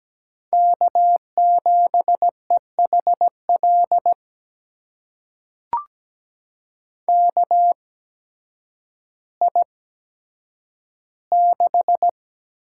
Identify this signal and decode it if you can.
Morse code, heard as a keyed tone, about 17 wpm, spaced as for 9 wpm, 700 Hz. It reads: K7EHL KI6
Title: USA Call Signs - 8x Character Spacing - 17wpm